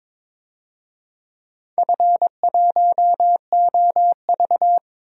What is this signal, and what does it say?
Morse code, heard as a keyed tone, about 22 wpm, 700 Hz. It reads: F1OV
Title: DXCC-100 Call Signs - 4 Characters - Repeated 3x - 22wpm